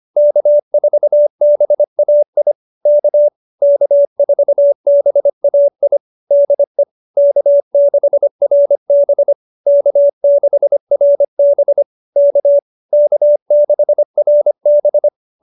Morse code, heard as a keyed tone, about 25 wpm, 590 Hz. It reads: K4BAI K K4BAI DE K6RB K6RB K K6RB